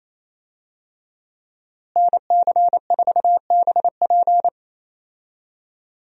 Morse code, heard as a keyed tone, about 28 wpm, 700 Hz. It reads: NC4BP
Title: USA Call Signs - 5 Characters - Repeated 2x - 28wpm